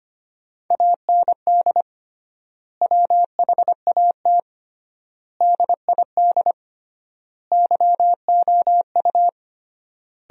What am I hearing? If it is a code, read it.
Morse code, heard as a keyed tone, about 25 wpm, 700 Hz. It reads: AND WHAT DID YOU
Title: 3x Word-Spacing - Sentences from Top 300 Words - 25wpm